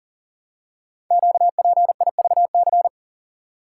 Morse code, heard as a keyed tone, about 42 wpm, 700 Hz. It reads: QPIVC
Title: Code Groups - 5 Letters - 40wpm